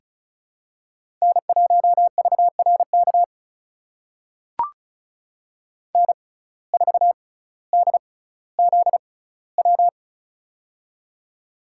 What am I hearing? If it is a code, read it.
Morse code, heard as a keyed tone, about 36 wpm, spaced as for 18 wpm, 700 Hz. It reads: N1VRK N4DZW
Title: USA Call Signs - 5 Characters - 6x Character Spacing - 35wpm